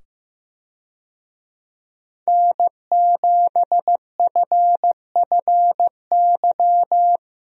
Morse code, heard as a keyed tone, about 15 wpm, 700 Hz. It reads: N7FFY
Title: USA Call Signs - 5 Characters - 15wpm